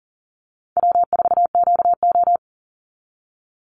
Morse code, heard as a keyed tone, about 40 wpm, 700 Hz. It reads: W4QO
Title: Easy Call Signs Repeated 2x - 40wpm